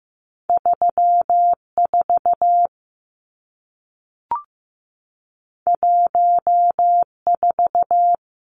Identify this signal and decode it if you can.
Morse code, heard as a keyed tone, about 15 wpm, 700 Hz. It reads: 3 4 1 4
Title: Numbers - 2-Digit 15wpm